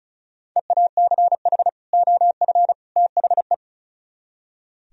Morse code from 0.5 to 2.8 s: EACH OF

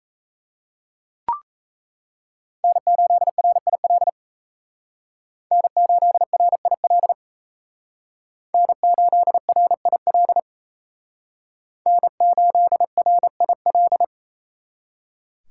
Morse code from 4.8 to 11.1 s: N8RIL N8RIL

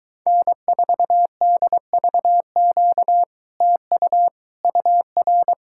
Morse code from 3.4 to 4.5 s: TU

